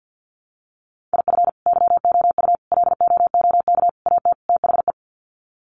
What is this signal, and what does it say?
Morse code, heard as a keyed tone, about 50 wpm, 700 Hz. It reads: IF YOU LOOK AT THE
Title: Sets of 5 Words 50wpm